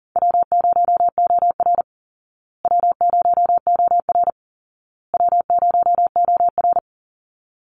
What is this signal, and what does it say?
Morse code, heard as a keyed tone, about 40 wpm, 700 Hz. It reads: W0OR W0OR W0OR
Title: Easy Call Signs Repeated 3x - 40wpm